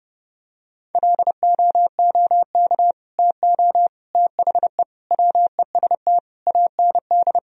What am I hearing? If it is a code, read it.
Morse code, heard as a keyed tone, about 30 wpm, 700 Hz. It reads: LOOK TO THE WEST AND
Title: Sentences from Top 1000 Words - Review Part 2 - 30wpm